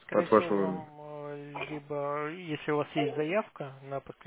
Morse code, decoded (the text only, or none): none